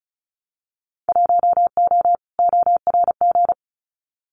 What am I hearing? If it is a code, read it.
Morse code, heard as a keyed tone, about 34 wpm, 700 Hz. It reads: 1O ORG
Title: Winter Field Day Exchanges 35wpm